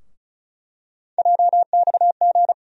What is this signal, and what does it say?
Morse code, heard as a keyed tone, about 34 wpm, 700 Hz. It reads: JXG
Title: USA Call Sign Postfixes - 3 Letters 35wpm